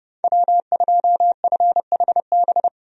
Morse code, heard as a keyed tone, about 32 wpm, 700 Hz. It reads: W2FHB
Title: USA Call Signs - 5 Characters - 30wpm